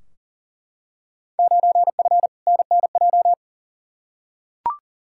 Morse code, heard as a keyed tone, about 40 wpm, 700 Hz. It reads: 9F NNJ
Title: Field Day Exchanges Encore - 40wpm